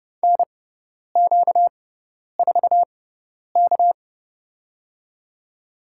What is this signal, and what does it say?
Morse code, heard as a keyed tone, about 30 wpm, spaced as for 10 wpm, 700 Hz. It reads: NQ4K
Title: USA Call Signs - 4 Characters - 6x Character Spacing - 30wpm